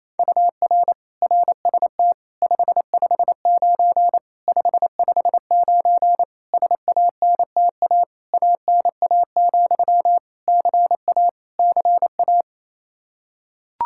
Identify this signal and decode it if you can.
Morse code, heard as a keyed tone, about 28 wpm, 700 Hz. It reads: UR RST 559 559 SANTA ANA, CA CA
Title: Practice QSO - Exchange 1 - Beginning - 28wpm